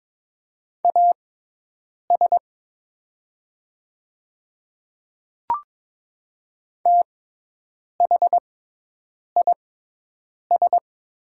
Morse code, heard as a keyed tone, about 22 wpm, spaced as for 8 wpm, 700 Hz. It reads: AS THIS
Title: 6x Character Spacing - Top 100 Words - 22wpm